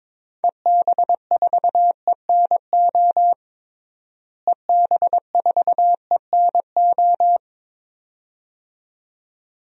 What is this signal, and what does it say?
Morse code, heard as a keyed tone, about 22 wpm, 700 Hz. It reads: EB4ENO EB4ENO
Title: DXCC-100 Call Signs - 6 Characters - Repeated 2x - 22wpm